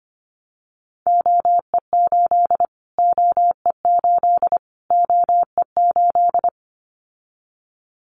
Morse code, heard as a keyed tone, about 25 wpm, 700 Hz. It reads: OE8 OE8 OE8
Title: Callsign Prefixes for Top 25 DXCC - Repeated 3x - 25wpm